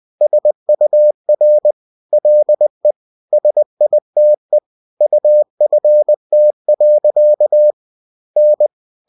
Morse code, heard as a keyed tone, about 20 wpm, 600 Hz. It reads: SUR LE SITE UFT. N